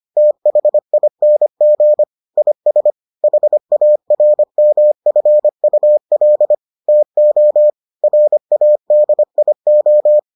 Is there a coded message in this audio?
Yes: Morse code from 0.2 to 10.2 s, heard as a keyed tone, about 25 wpm, 600 Hz. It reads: THING IS HARMFUL TO RADIO